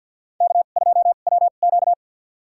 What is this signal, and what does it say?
Morse code, heard as a keyed tone, about 47 wpm, spaced as for 35 wpm, 700 Hz. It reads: K2WQ
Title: USA Call Signs - 4 Characters - 2x Character Spacing - 50wpm